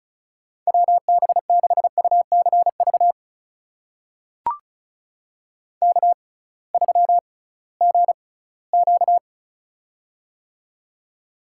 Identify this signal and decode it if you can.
Morse code, heard as a keyed tone, about 36 wpm, spaced as for 20 wpm, 700 Hz. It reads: WB6UCV K3GQ
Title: USA Call Signs - 6x Character Spacing - 35wpm